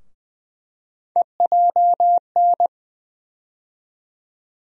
Morse code, heard as a keyed tone, about 20 wpm, 700 Hz. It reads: EJN
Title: USA Call Sign Postfixes - 3 Letters 20wpm